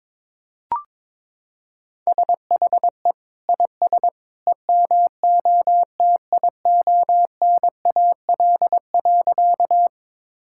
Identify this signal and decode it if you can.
Morse code, heard as a keyed tone, about 21 wpm, 700 Hz. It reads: SHE IS EMOTIONAL.
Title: Sentences from Top 1000 Words - Review Part 5 - 22wpm